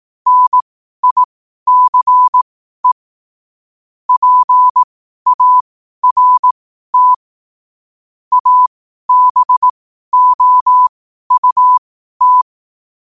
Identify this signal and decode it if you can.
Morse code, heard as a keyed tone, about 18 wpm, spaced as for 12 wpm, 1000 Hz. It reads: NICE PART ABOUT